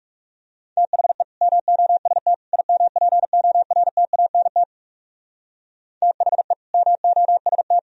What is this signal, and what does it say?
Morse code, heard as a keyed tone, about 35 wpm, 700 Hz. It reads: THE MOST IMPORTANT THE MOST
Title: Speed-Racing - Sets of 3 Words 30wpm